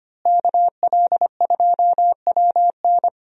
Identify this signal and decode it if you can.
Morse code, heard as a keyed tone, about 25 wpm, 700 Hz. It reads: KL2WN
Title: USA Call Signs - 5 Characters - Repeated 3x - 25wpm